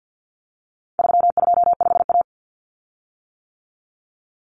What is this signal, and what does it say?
Morse code, heard as a keyed tone, about 50 wpm, 700 Hz. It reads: 325A
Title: Common Cut Numbers - 4-Digit 50wpm